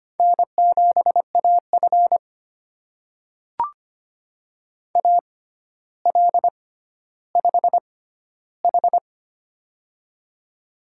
Morse code, heard as a keyed tone, about 25 wpm, spaced as for 13 wpm, 700 Hz. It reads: N7AF AL5H
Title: USA Call Signs - 4 Characters - 6x Character Spacing - 25wpm